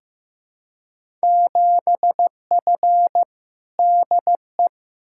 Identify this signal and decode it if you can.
Morse code, heard as a keyed tone, about 16 wpm, 700 Hz. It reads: 7F DE
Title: Field Day Exchanges Encore - 15wpm